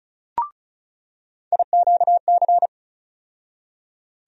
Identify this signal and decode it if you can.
Morse code, heard as a keyed tone, about 34 wpm, 700 Hz. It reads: IQC
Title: USA Call Sign Postfixes - 3 Letters 35wpm